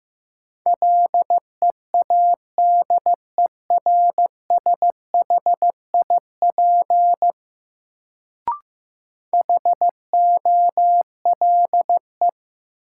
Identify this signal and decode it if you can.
Morse code, heard as a keyed tone, about 15 wpm, 700 Hz. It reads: LEADERSHIP HOLE
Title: Sentences from Top 1600 Words - New Word Drill - 15wpm